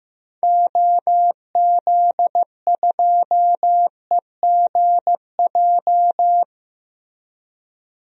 Morse code, heard as a keyed tone, about 15 wpm, 700 Hz. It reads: OZ2EGJ
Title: DXCC-100 Call Signs - 6 Characters - Repeated 2x - 15wpm